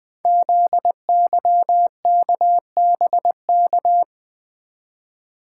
Morse code, heard as a keyed tone, about 20 wpm, 700 Hz. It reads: ZYKBK
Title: Code Groups - 5 Letters - 20wpm